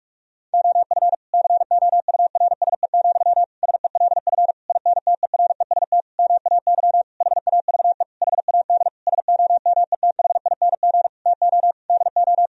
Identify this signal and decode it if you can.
Morse code, heard as a keyed tone, about 46 wpm, 700 Hz. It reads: OF COURSE, SELF INTEREST MAY HAVE HAD SOMETHING TO DO
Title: Sentences from Top 1000 Words - Review Part 2 - 45wpm